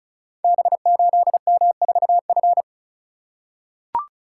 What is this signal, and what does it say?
Morse code, heard as a keyed tone, about 36 wpm, 690 Hz. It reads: B8M4F
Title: Code Groups - 5 Letter:Numbers - 35wpm